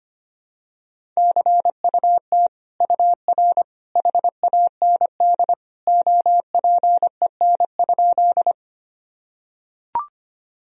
Morse code, heard as a keyed tone, about 25 wpm, 700 Hz. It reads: CUT UR HAND OPEN?
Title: Sentences from Top 300 Words and Abbreviations 25wpm